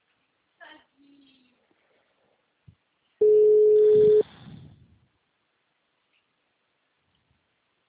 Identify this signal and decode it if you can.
Ringback tone (Europe, 425 Hz), with speech.
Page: Pavuna - RJ